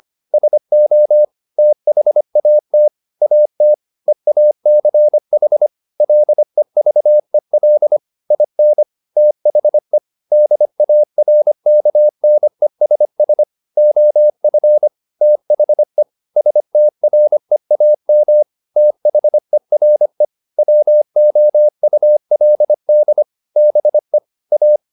Morse code, heard as a keyed tone, about 25 wpm, 600 Hz. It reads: SO THAT AT EACH LEVEL IN THE DARKNESS OF THE STREAM THERE WOULD BE A
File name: the old man and the sea part 1 25wpm